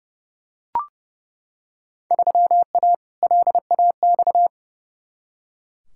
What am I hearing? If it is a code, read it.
Morse code, heard as a keyed tone, about 30 wpm, 700 Hz. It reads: 3A LAX
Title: Field Day Exchanges - 30wpm